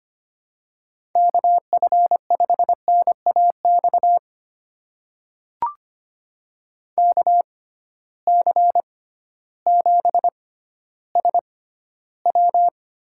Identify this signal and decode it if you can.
Morse code, heard as a keyed tone, about 25 wpm, spaced as for 13 wpm, 700 Hz. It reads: KF5NAX KC7SW
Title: USA Call Signs - 6 Characters - 6x Character Spacing - 25wpm